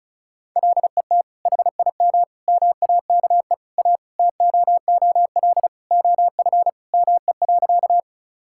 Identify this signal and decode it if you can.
Morse code, heard as a keyed tone, about 34 wpm, 700 Hz. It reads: LET HIM MAKE A TOOL OF ME.
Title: Sentences from Top 1000 Words - Review Part 6 - 35wpm